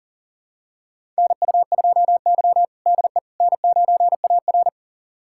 Morse code, heard as a keyed tone, about 40 wpm, 700 Hz. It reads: NU2Y DE N9AR
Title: USA Callsign DE Callsign - 4 Characters - 40wpm